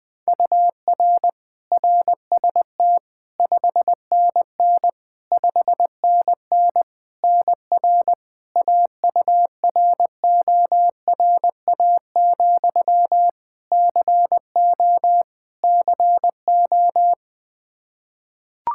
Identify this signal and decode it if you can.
Morse code, heard as a keyed tone, about 20 wpm, 700 Hz. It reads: UR RST 5NN 5NN NR AURORA, CO CO
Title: Practice QSO - Exchange 1 - Beginning - 20wpm